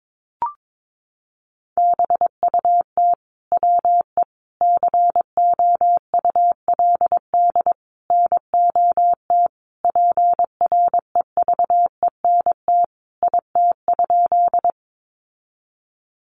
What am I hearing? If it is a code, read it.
Morse code, heard as a keyed tone, about 22 wpm, 700 Hz. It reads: BUT WE COULD NOT PREVENT IT?
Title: Sentences from Top 1100 Words 22wpm